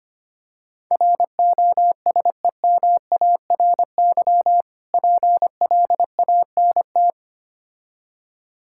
Morse code, heard as a keyed tone, about 25 wpm, 690 Hz. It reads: ROSEMARY PLANT
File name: practice / 25wpm / Difficult Words with Hints - 25wpm